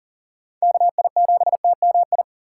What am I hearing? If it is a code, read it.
Morse code, heard as a keyed tone, about 40 wpm, 700 Hz. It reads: KI7TMI
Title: USA Call Signs - 6 Characters - 4x Character Spacing - 40wpm